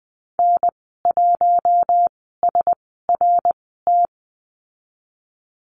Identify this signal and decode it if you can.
Morse code, heard as a keyed tone, about 20 wpm, spaced as for 15 wpm, 700 Hz. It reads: N1SRT
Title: 069 Lesson 12g - Letter d - Call Signs 20wpm